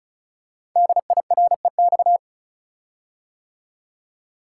Morse code, heard as a keyed tone, about 36 wpm, 690 Hz. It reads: DIREX
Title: Code Groups - 5 Letters - 35wpm